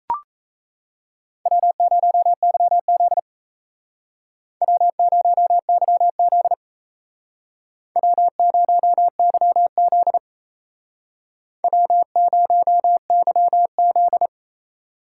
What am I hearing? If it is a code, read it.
Morse code, heard as a keyed tone, about 35 wpm, 700 Hz. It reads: W0YZ W0YZ W0YZ W0YZ